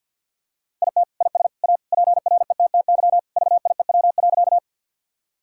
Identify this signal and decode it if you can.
Morse code, heard as a keyed tone, about 52 wpm, 700 Hz. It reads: IT IS A PRETTY VIEW.
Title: Sentences from Top 1000 Words - Review Part 6 - 50wpm